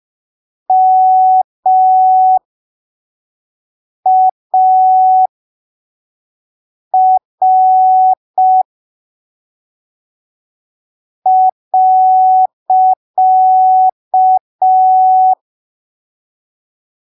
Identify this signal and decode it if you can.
Morse code, heard as a keyed tone, about 5 wpm, spaced as for 3 wpm, 750 Hz. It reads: MAR.
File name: arquivos / texto / texto 03 5ppm